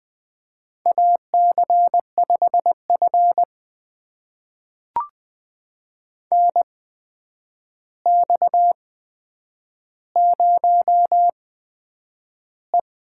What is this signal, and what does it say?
Morse code, heard as a keyed tone, about 20 wpm, spaced as for 9 wpm, 700 Hz. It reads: AC5F NX0E
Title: USA Call Signs - 4 Characters - 8x Character Spacing - 20wpm